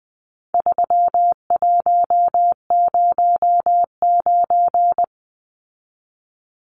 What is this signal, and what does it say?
Morse code, heard as a keyed tone, about 20 wpm, 700 Hz. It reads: 3109
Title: Numbers - 4-Digit 20wpm